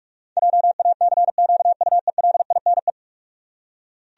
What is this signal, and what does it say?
Morse code, heard as a keyed tone, about 47 wpm, 700 Hz. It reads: JACQUELINE
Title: USA Common Names - 45wpm